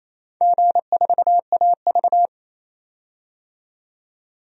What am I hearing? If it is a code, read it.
Morse code, heard as a keyed tone, about 28 wpm, 700 Hz. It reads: G4AV
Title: DXCC-100 Call Signs - 4 Characters - 28wpm